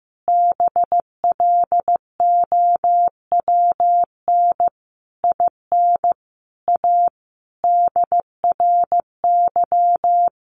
Morse code, heard as a keyed tone, about 15 wpm, 700 Hz. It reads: BLOWN IN A DRY
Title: A Sound Of Thunder 15wpm